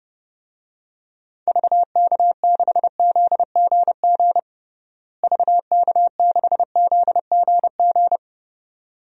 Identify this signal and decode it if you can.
Morse code, heard as a keyed tone, about 29 wpm, 700 Hz. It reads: VK6ZGG VK6ZGG
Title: DXCC-100 Call Signs - 6 Characters - Repeated 2x - 30wpm